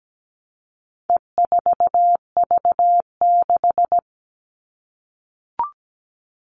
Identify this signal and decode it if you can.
Morse code, heard as a keyed tone, about 17 wpm, 700 Hz. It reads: E4V6
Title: All Cut Numbers - 4-Digit 17wpm